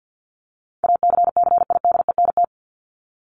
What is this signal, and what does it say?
Morse code, heard as a keyed tone, about 48 wpm, 700 Hz. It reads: ACCIDENT
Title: Top 1751-2000 Words 50wpm